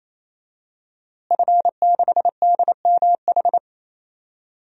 Morse code, heard as a keyed tone, about 28 wpm, 700 Hz. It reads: F6DMH
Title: DXCC-100 Call Signs - 5 Characters - Repeated 3x - 28wpm